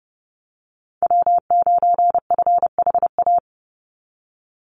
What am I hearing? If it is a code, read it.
Morse code, heard as a keyed tone, about 28 wpm, 700 Hz. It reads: W9FHA
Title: NA SOTA Chaser Callsigns 30wpm